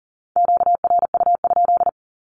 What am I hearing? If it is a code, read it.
Morse code, heard as a keyed tone, about 40 wpm, 700 Hz. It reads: QRU?
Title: Q-Codes with Spoken Meaning - 40wpm